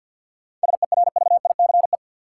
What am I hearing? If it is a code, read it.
Morse code, heard as a keyed tone, about 47 wpm, 700 Hz. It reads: SERVICE
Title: Sentences from Top 700 Words - New Word Drill - 50wpm